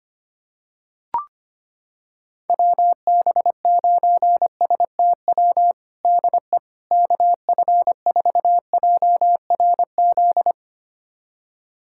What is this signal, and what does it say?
Morse code, heard as a keyed tone, about 25 wpm, 700 Hz. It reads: WB9STW DE KF4JRZ
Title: USA Callsign DE Callsign 25wpm